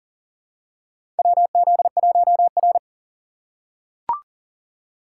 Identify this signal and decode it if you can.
Morse code, heard as a keyed tone, about 40 wpm, 700 Hz. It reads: WZ1R
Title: USA Call Signs - 4 Characters - 4x Character Spacing - 40wpm